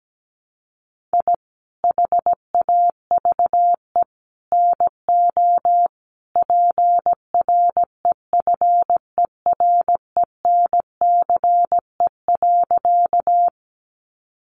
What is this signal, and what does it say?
Morse code, heard as a keyed tone, about 17 wpm, 700 Hz. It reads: I HAVE NO PREFERENCE.
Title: Sentences from Top 1500 Words 17wpm